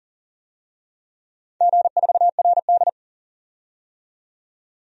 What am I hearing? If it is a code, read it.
Morse code, heard as a keyed tone, about 40 wpm, 700 Hz. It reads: G4RD
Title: DXCC-100 Call Signs - 4 Characters - 40wpm